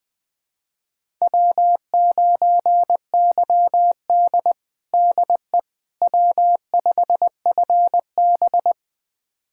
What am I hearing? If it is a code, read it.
Morse code, heard as a keyed tone, about 20 wpm, 700 Hz. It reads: W9YD DE W5FB
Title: USA Callsign DE Callsign - 4 Characters - 20wpm